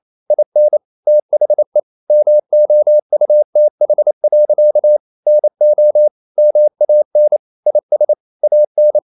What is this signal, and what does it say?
Morse code, heard as a keyed tone, about 28 wpm, 600 Hz. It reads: IN THE MOUTH. NO MAN IS AN
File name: CW / Practice / EnglishSayings28